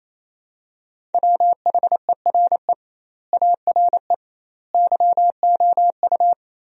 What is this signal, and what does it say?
Morse code, heard as a keyed tone, about 28 wpm, 700 Hz. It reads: WHERE ARE YOU
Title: 2x Word-Spacing - Sentences from Top 200 Words - 28wpm